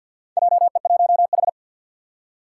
Morse code, heard as a keyed tone, about 53 wpm, 700 Hz. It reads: JE1H
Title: Four Challenging Letter-Number Combinations 50wpm